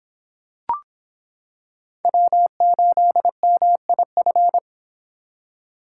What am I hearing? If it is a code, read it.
Morse code, heard as a keyed tone, about 25 wpm, 700 Hz. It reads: W8MIF